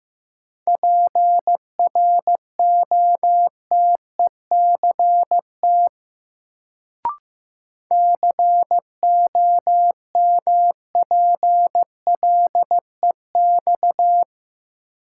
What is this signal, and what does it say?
Morse code, heard as a keyed tone, about 15 wpm, 700 Hz. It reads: PROTECT COMPLEX
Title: Sentences from Top 900 Words - New Word Drill - 15wpm